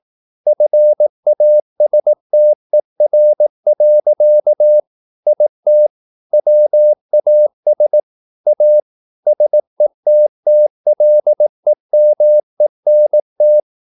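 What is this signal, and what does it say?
Morse code, heard as a keyed tone, about 18 wpm, 600 Hz. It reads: FASTER. IT WAS A SETTLEMENT
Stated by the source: Morse practice at 18 wpm, English text, from CW med Gnister.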